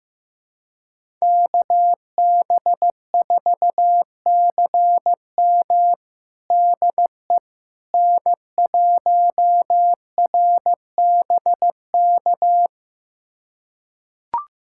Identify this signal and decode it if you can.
Morse code, heard as a keyed tone, about 15 wpm, 700 Hz. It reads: KB4CM DE N1RBK
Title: USA Callsign DE Callsign - 5 Characters - 15wpm